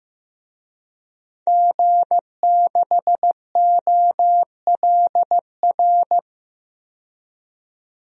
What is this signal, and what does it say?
Morse code, heard as a keyed tone, about 15 wpm, 700 Hz. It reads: G6OLR
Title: DXCC-100 Call Signs - 5 Characters - 15wpm